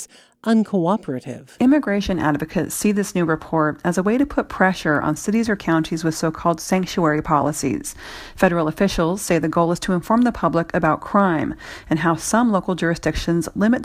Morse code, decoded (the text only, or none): none